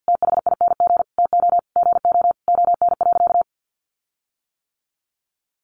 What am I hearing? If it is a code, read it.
Morse code, heard as a keyed tone, about 50 wpm, 700 Hz. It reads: THING TO GO ON.
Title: Sentences from Top 1300 Words 50wpm